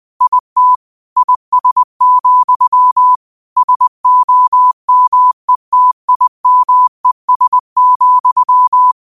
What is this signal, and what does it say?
Morse code, heard as a keyed tone, about 20 wpm, 1000 Hz. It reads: IT IS, SOMETIMES,